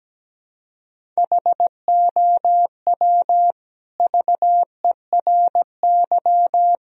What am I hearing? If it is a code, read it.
Morse code, heard as a keyed tone, about 17 wpm, 700 Hz. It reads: HOW VERY
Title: Sentences from Top 1000 Words - Review Part 1 - 17wpm